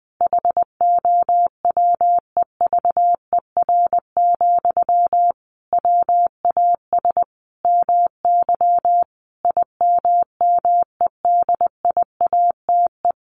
Morse code, heard as a keyed tone, about 19 wpm, 700 Hz. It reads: HOWEVER, WAS MY IMMEDIATE
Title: Sentences from Top 1500 Words 20wpm